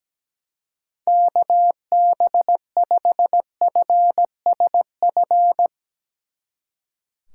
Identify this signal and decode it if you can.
Morse code, heard as a keyed tone, about 17 wpm, 700 Hz. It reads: KB5FSF
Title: Speed Racing - USA 2x3 Call Signs 17wpm